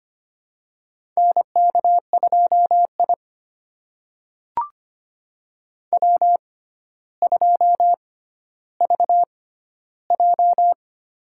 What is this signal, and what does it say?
Morse code, heard as a keyed tone, about 25 wpm, spaced as for 13 wpm, 700 Hz. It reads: NK2I W2VJ
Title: USA Call Signs - 4 Characters - 6x Character Spacing - 25wpm